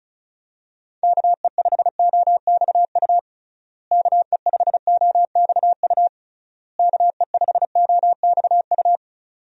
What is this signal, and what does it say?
Morse code, heard as a keyed tone, about 36 wpm, 700 Hz. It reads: KE5OXU KE5OXU KE5OXU
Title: USA Call Signs - 6 Characters - Repeated 3x - 35wpm